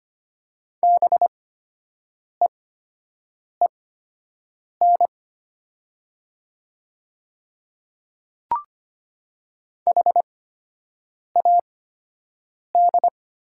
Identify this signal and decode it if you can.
Morse code, heard as a keyed tone, about 25 wpm, spaced as for 7 wpm, 700 Hz. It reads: BEEN HAD
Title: 8x Character Spacing - Top 100 Words - 25wpm